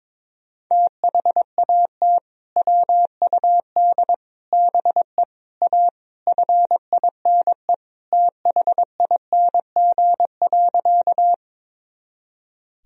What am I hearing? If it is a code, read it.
Morse code, heard as a keyed tone, about 22 wpm, 700 Hz. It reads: THAT WUD BE A FINE THING.
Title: Sentences from Top 500 Words and Abbreviations 22wpm